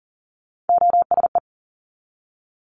Morse code, heard as a keyed tone, about 42 wpm, 700 Hz. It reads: OSE
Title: Top Three Letter Combinations in Words 40wpm